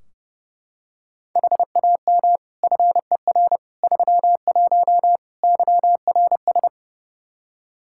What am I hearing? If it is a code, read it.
Morse code, heard as a keyed tone, about 32 wpm, 700 Hz. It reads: HAM FER 31 YRS
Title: K1USN SST QSO Practice 30wpm